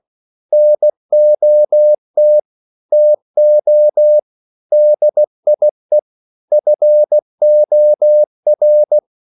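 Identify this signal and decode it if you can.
Morse code, heard as a keyed tone, about 16 wpm, 600 Hz. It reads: NOT TO DIE FOR
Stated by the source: English quotations sent in CW as practice at 16 words per minute.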